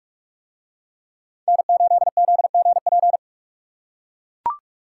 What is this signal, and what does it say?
Morse code, heard as a keyed tone, about 44 wpm, 700 Hz. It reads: N8ZGP